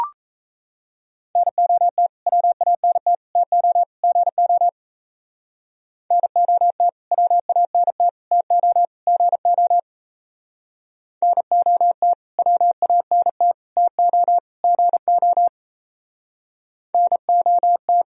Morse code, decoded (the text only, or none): NOT WANT TO GO NOT WANT TO GO NOT WANT TO GO NOT